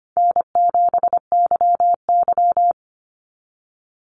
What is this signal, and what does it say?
Morse code, heard as a keyed tone, about 25 wpm, 700 Hz. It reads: N7YY
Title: NA SOTA Chaser Callsigns 25wpm